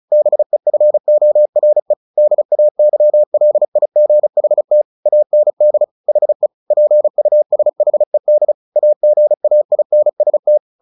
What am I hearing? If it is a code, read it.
Morse code, heard as a keyed tone, about 35 wpm, 600 Hz. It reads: BEFORE DAYLIGHT AND HE PUSHED AGAINST
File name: the old man and the sea / the old man and the sea 1 35wpm